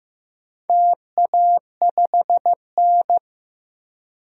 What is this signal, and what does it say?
Morse code, heard as a keyed tone, about 15 wpm, 700 Hz. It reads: TA5N
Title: DXCC-100 Call Signs - 4 Characters - 15wpm